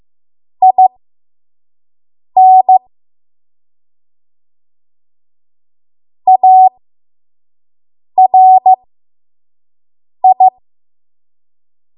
Morse code, heard as a keyed tone, about 15 wpm, spaced as for 5 wpm, 750 Hz. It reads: IN ARI
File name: Archive / 5 WPM / 05WPM